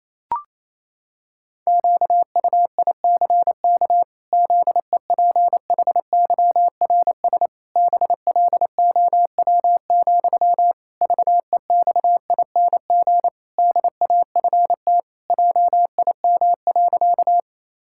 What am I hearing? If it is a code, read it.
Morse code, heard as a keyed tone, about 28 wpm, 700 Hz. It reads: QUICK ZEPHYRS BLOW, VEXING DAFT JIM.
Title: Pangram Challenge v2 - 28wpm